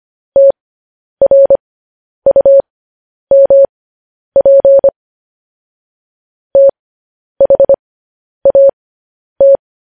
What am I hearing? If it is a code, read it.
Morse code, heard as a keyed tone, about 25 wpm, spaced as for 10 wpm, 550 Hz. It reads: TRUMP THAT
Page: news-headlines-in-morse-code-at-10-wpm